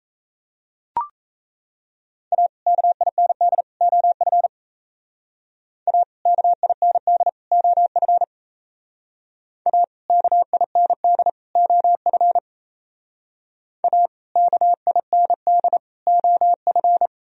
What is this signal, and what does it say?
Morse code, heard as a keyed tone, about 36 wpm, 700 Hz. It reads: A KIND OF A KIND OF A KIND OF A KIND OF